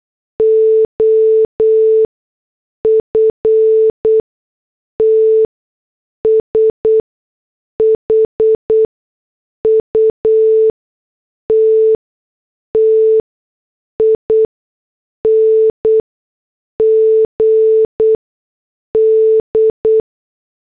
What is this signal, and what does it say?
Morse code, heard as a keyed tone, about 8 wpm, spaced as for 6 wpm, 440 Hz. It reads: OFT SHUTTING D
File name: morse / headlines